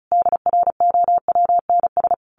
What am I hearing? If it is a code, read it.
Morse code, heard as a keyed tone, about 34 wpm, 700 Hz. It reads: DROWNS
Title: A Sound Of Thunder 35wpm